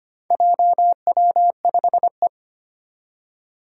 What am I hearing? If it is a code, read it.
Morse code, heard as a keyed tone, about 25 wpm, 700 Hz. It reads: JW5E
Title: Four Challenging Letter-Number Combinations 25wpm